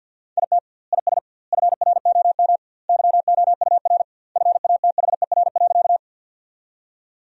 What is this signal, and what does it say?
Morse code, heard as a keyed tone, about 50 wpm, 700 Hz. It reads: IT IS FROM YOUR FATHER.
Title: 2x Word-Spacing - Sentences from Top 300 Words - 50wpm